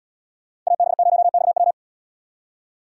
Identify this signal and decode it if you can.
Morse code, heard as a keyed tone, about 69 wpm, 700 Hz. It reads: AB0XW